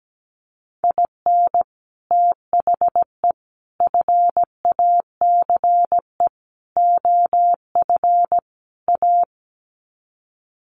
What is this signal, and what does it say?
Morse code, heard as a keyed tone, about 17 wpm, 700 Hz. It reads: IN THE FACE OF A
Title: Sets of 5 Words Encore 17wpm